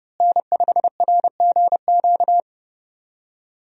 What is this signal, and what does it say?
Morse code, heard as a keyed tone, about 30 wpm, 700 Hz. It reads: N5RGQ
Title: USA Call Signs - 5 Characters - Repeated 3x - 30wpm